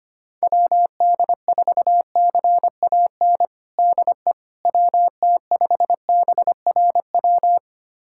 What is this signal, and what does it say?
Morse code, heard as a keyed tone, about 25 wpm, 700 Hz. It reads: WD4CAN DE WT5BRW
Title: USA Callsign DE Callsign 25wpm